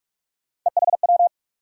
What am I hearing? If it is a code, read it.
Morse code, heard as a keyed tone, about 46 wpm, 700 Hz. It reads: EHW
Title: USA Call Sign Postfixes - 3 Letters 45wpm